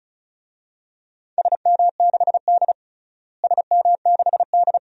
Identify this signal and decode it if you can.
Morse code, heard as a keyed tone, about 36 wpm, 700 Hz. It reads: SM6D SM6D
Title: DXCC-100 Call Signs - 4 Characters - Repeated 2x - 35wpm